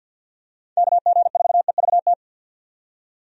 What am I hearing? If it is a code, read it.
Morse code, heard as a keyed tone, about 51 wpm, 700 Hz. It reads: KG4EVT
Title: USA Call Signs - 6 Characters - 50wpm